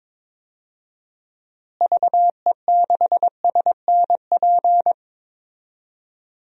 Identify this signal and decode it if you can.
Morse code, heard as a keyed tone, about 22 wpm, 700 Hz. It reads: VE6SNP
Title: DXCC-100 Call Signs - 6 Characters - Repeated 2x - 22wpm